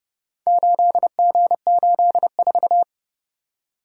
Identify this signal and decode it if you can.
Morse code, heard as a keyed tone, about 28 wpm, 700 Hz. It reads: 8G84